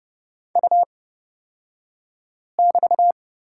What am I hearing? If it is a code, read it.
Morse code, heard as a keyed tone, about 30 wpm, spaced as for 5 wpm, 700 Hz. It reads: U<BT>
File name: Lesson 37d - Letter x - Two Characters 30wpm